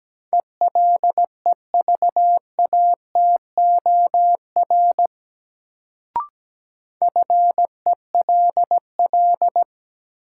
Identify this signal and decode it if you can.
Morse code, heard as a keyed tone, about 17 wpm, 700 Hz. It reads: ELEVATOR FELL
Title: Sentences from Top 1600 Words - New Word Drill Alt - 17wpm